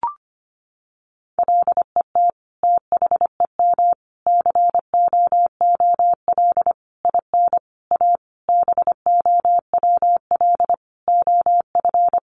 Morse code, heard as a keyed tone, about 25 wpm, 700 Hz. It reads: LET THEM COOL IN A BOWL OF
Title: Sentences from Top 1200 Words 25wpm